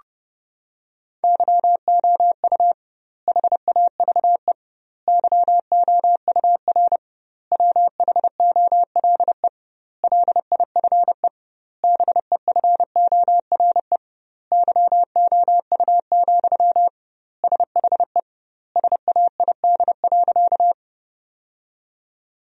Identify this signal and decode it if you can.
Morse code, heard as a keyed tone, about 32 wpm, 700 Hz. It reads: YOU HAVE YOUR WHOLE LIFE BEFORE YOU, SHE SAID.
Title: 2x Word-Spacing - Sentences from Top 400 Words - 30wpm